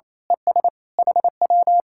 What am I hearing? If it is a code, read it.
Morse code, heard as a keyed tone, about 26 wpm, 700 Hz. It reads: ES HW
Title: Speed-Racing Common QSO Phrases - 28wpm